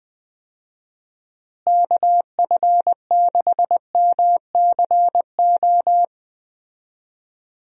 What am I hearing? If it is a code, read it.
Morse code, heard as a keyed tone, about 21 wpm, 700 Hz. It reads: KF6MCO
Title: USA Call Signs - 6 Characters - Repeated 3x - 20wpm